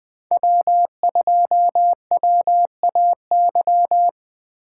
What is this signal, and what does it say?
Morse code, heard as a keyed tone, about 20 wpm, 700 Hz. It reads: W2WAY
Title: USA Call Signs - 5 Characters - 8x Character Spacing - 20wpm